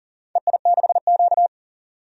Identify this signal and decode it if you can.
Morse code, heard as a keyed tone, about 40 wpm, 690 Hz. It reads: EI6Q
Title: DXCC-100 Call Signs - 4 Characters - 40wpm